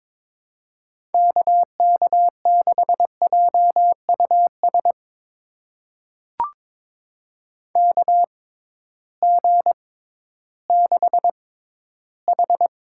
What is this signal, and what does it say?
Morse code, heard as a keyed tone, about 22 wpm, spaced as for 13 wpm, 700 Hz. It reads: KK6JUS KG6H
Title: USA Call Signs - 6x Character Spacing - 22wpm